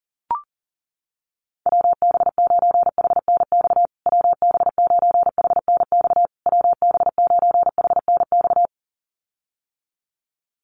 Morse code, heard as a keyed tone, about 40 wpm, 700 Hz. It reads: WB9HNX WB9HNX WB9HNX
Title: USA Call Signs Repeated 3x 40wpm